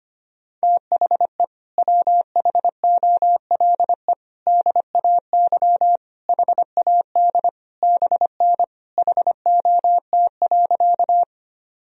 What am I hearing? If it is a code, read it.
Morse code, heard as a keyed tone, about 25 wpm, 700 Hz. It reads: THE WHOLE DAY HAD BN HOT.
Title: Sentences from Top 500 Words and Abbreviations 25wpm